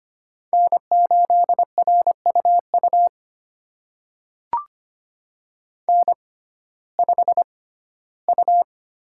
Morse code, heard as a keyed tone, about 25 wpm, spaced as for 15 wpm, 700 Hz. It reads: N8RUU N5U
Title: USA Call Signs - 5 Characters - 6x Character Spacing - 25wpm